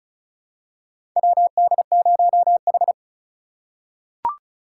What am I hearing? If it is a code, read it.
Morse code, heard as a keyed tone, about 36 wpm, 700 Hz. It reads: WD0H
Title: USA Call Signs - 4 Characters - 3x Character Spacing - 35wpm